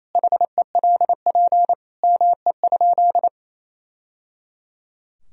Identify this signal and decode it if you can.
Morse code, heard as a keyed tone, about 28 wpm, 700 Hz. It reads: HELP ME?
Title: Sentences from Top 1000 Words - Review Part 3 - 28wpm